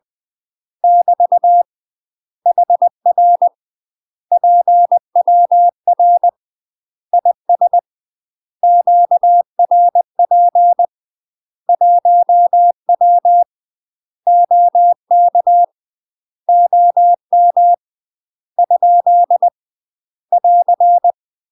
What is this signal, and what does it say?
Morse code, heard as a keyed tone, about 20 wpm, 700 Hz. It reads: <BT> HR PWR IS QRP 1W OK OM ? <AR>
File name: UK MASTER QSOs TRAINING / 20WPM / UK MASTER QSOs 0051